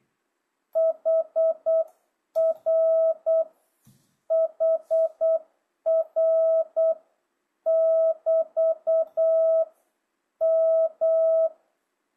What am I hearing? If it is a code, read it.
Morse code, heard as a keyed tone, about 8 wpm, 660 Hz. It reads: HR HR <BT> M
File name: jh8cbh / 2025cw